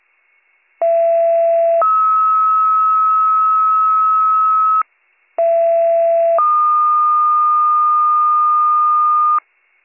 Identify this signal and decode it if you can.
Two-tone page, heard as a one-second tone, then a three-second tone, about 670 then 1290 Hz, about 670 then 1220 Hz. It is heard from 0.8 to 9.4 s.